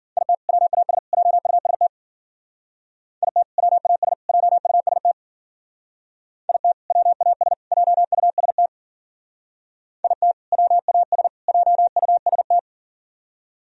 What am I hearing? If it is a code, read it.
Morse code, heard as a keyed tone, about 51 wpm, 700 Hz. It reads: IT WAS JUST IT WAS JUST IT WAS JUST IT WAS JUST